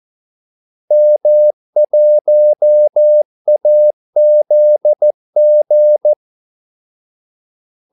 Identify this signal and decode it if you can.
Morse code, heard as a keyed tone, about 14 wpm, 600 Hz. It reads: M1AZG
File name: Callsigns-14wpm